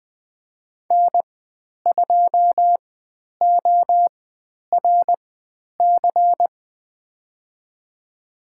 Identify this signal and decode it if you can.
Morse code, heard as a keyed tone, about 20 wpm, spaced as for 10 wpm, 700 Hz. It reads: N2ORC
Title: 102 Lesson 17g - Letter u - Call Signs 20wpm